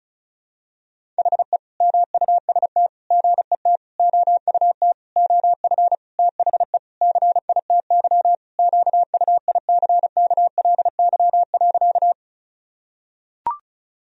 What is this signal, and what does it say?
Morse code, heard as a keyed tone, about 35 wpm, 700 Hz. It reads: HE MUST GET OUT OF THE CITY QUICKLY.
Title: Sentences from Top 1000 Words - Review Part 1 - 35wpm